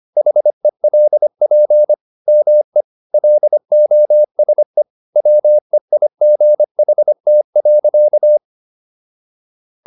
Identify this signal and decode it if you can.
Morse code, heard as a keyed tone, about 25 wpm, 600 Hz. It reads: HELP ME LOSE WEIGHT.